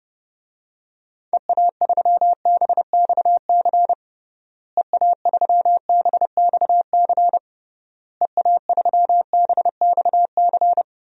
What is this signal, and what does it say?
Morse code, heard as a keyed tone, about 31 wpm, 700 Hz. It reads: EA3BXC EA3BXC EA3BXC
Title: DXCC-100 Call Signs - 6 Characters - Repeated 3x - 30wpm